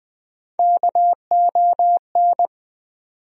Morse code, heard as a keyed tone, about 20 wpm, 700 Hz. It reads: KON
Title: USA Call Sign Postfixes - 3 Letters 20wpm